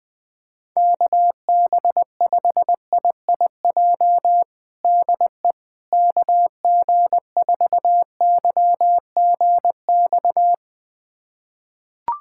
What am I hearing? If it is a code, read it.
Morse code, heard as a keyed tone, about 20 wpm, 700 Hz. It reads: KB5IIJ DE KG4YGX
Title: USA Callsign DE Callsign - 6 Characters - 20wpm